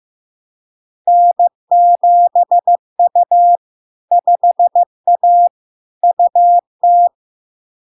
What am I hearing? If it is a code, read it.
Morse code, heard as a keyed tone, about 15 wpm, 700 Hz. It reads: N7U 5A UT
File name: LARK-FD-2022-Log-in-CW-15wpm-700-Hz